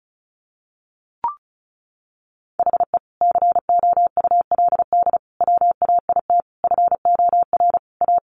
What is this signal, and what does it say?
Morse code, heard as a keyed tone, about 34 wpm, 700 Hz. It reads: HE COULD WAIT FOR A
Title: Sentences from Top 1300 Words 35wpm